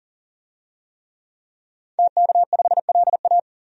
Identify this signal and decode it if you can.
Morse code, heard as a keyed tone, about 38 wpm, 700 Hz. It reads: TK5LA
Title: DXCC-100 Call Signs - 5 Characters - 40wpm